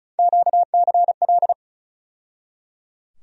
Morse code, heard as a keyed tone, about 36 wpm, 700 Hz. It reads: QCL
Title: USA Call Sign Postfixes - 3 Letters 35wpm